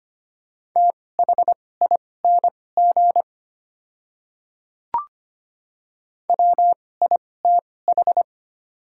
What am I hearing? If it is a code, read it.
Morse code, heard as a keyed tone, about 25 wpm, spaced as for 18 wpm, 700 Hz. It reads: THING WITH
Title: 2x Character Spacing - Top 100 Words - 25wpm